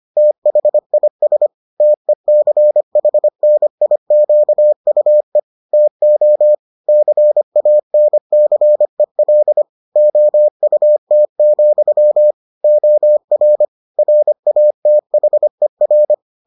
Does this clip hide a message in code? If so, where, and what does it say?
Yes: Morse code from 0.2 to 16.1 s, heard as a keyed tone, about 25 wpm, 600 Hz. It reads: THIS TECHNIQUE TO CANCEL OUT, OR RATHER